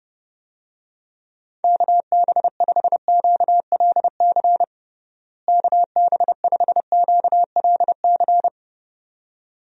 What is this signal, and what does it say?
Morse code, heard as a keyed tone, about 30 wpm, 700 Hz. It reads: KB5QLC KB5QLC
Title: USA Call Signs - 6 Characters - Repeated 2x - 30wpm